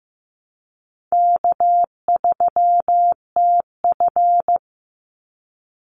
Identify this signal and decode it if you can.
Morse code, heard as a keyed tone, about 15 wpm, 700 Hz. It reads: K3TF
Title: Easy Call Signs Repeated 2x - 15wpm